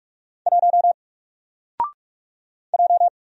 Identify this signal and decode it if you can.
Morse code, heard as a keyed tone, about 45 wpm, 700 Hz. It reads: 1 J
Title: Single Letter-Number 45wpm